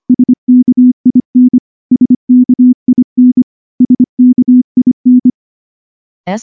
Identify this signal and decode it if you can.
Morse code, heard as a keyed tone, about 24 wpm, 250 Hz. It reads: SKIN SKIN SKIN
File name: morse / morse-training-25WPM